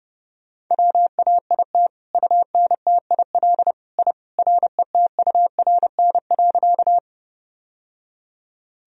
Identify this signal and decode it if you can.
Morse code, heard as a keyed tone, about 29 wpm, 700 Hz. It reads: WAIT UNTIL I RETURN.
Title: Sentences from Top 1000 Words - Review Part 1 - 30wpm